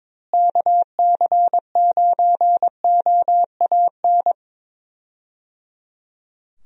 Morse code, heard as a keyed tone, about 22 wpm, 700 Hz. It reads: KC9OAN
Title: USA Call Signs - 6 Characters - Repeated 3x - 22wpm